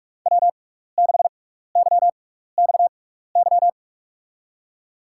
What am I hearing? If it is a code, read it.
Morse code, heard as a keyed tone, about 46 wpm, spaced as for 16 wpm, 700 Hz. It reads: W6YXY
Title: USA Call Signs - 5 Characters - 6x Character Spacing - 45wpm